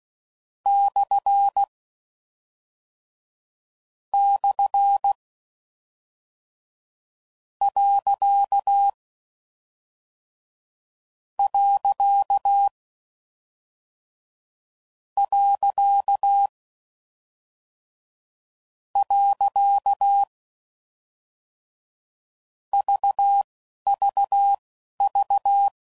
Morse code, heard as a keyed tone, about 16 wpm, spaced as for 11 wpm, 800 Hz. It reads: / / . . VVV